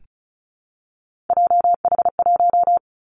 Morse code, heard as a keyed tone, about 34 wpm, 700 Hz. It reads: JH1
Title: Callsign Prefixes for Top 25 DXCC - 35wpm